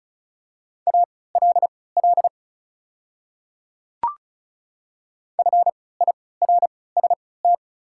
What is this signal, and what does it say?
Morse code, heard as a keyed tone, about 36 wpm, spaced as for 20 wpm, 690 Hz. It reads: ALL FIRST
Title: 3x Character Spacing - Top 100 Words - 35wpm